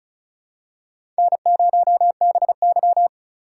Morse code, heard as a keyed tone, about 36 wpm, 700 Hz. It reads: N0BY
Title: USA Call Signs - 8x Character Spacing - 35wpm